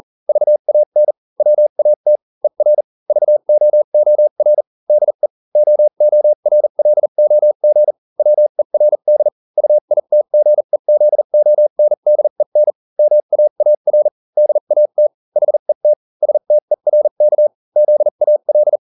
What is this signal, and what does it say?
Morse code, heard as a keyed tone, about 40 wpm, spaced as for 35 wpm, 600 Hz. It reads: VAN WAT ER VOOR DE OORLOG WERD UITGEZONDEN MAAR DAT HET STERK ZAL